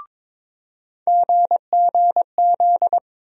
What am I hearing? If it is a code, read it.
Morse code, heard as a keyed tone, about 22 wpm, 700 Hz. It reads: GGZ
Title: USA Call Sign Postfixes - 3 Letters 22wpm